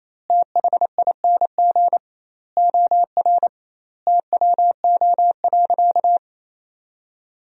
Morse code, heard as a keyed tone, about 28 wpm, 700 Hz. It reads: THING OR TWO.
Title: 2x Word-Spacing - Sentences from Top 300 Words - 28wpm